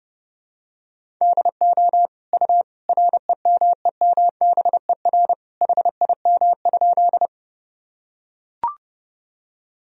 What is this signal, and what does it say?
Morse code, heard as a keyed tone, about 32 wpm, 700 Hz. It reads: DO U REMEMBER HIM?
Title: Sentences from Top 400 Words and Abbreviations 30wpm